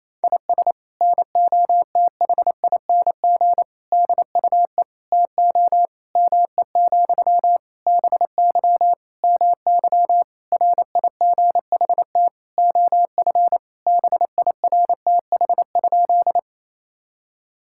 Morse code, heard as a keyed tone, about 28 wpm, 700 Hz. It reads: IS NOTHING DUE TO ME, BY MY RIGHT OF BIRTH?
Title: Sentences from Top 1000 Words - Review Part 6 - 28wpm